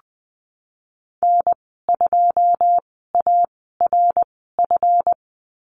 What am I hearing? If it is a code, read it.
Morse code, heard as a keyed tone, about 20 wpm, spaced as for 15 wpm, 700 Hz. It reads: N2ARF
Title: 154 Lesson 25g - Letter p - Call Signs 20wpm